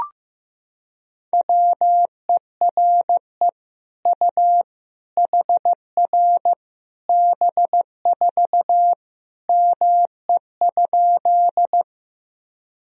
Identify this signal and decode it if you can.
Morse code, heard as a keyed tone, about 15 wpm, 700 Hz. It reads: WERE U HR B4 ME?
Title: Sentences from Top 200 Words and Abbreviations 15wpm